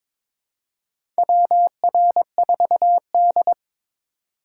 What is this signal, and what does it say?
Morse code, heard as a keyed tone, about 22 wpm, 700 Hz. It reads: WR4D
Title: USA Call Signs - 4 Characters - 2x Character Spacing - 22wpm